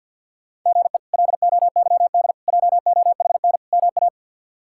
Morse code, heard as a keyed tone, about 53 wpm, 700 Hz. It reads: GE LOYD JOHN MA
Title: K1USN SST QSO Practice - 50wpm